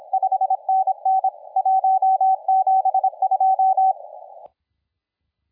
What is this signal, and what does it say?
Morse code, heard as a keyed tone, about 26 wpm, 760 Hz, with radio noise. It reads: 5NN 172
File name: la4c / sac-cw-2017 / CONTEST AUDIO 1100Z